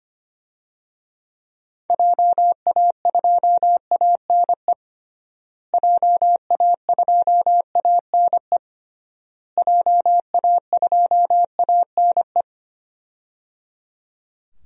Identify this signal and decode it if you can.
Morse code, heard as a keyed tone, about 25 wpm, 700 Hz. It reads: JA2ANE JA2ANE JA2ANE
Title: DXCC-100 Call Signs - 6 Characters - Repeated 3x - 25wpm